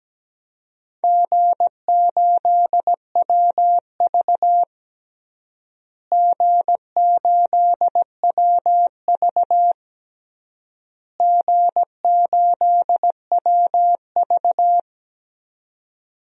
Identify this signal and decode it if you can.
Morse code, heard as a keyed tone, about 17 wpm, 700 Hz. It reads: G8WV G8WV G8WV
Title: DXCC-100 Call Signs - 4 Characters - Repeated 3x - 17wpm